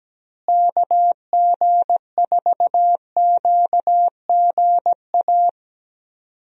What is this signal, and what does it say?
Morse code, heard as a keyed tone, about 17 wpm, 700 Hz. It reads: KG4QGA
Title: USA Call Signs - 6 Characters - Repeated 3x - 17wpm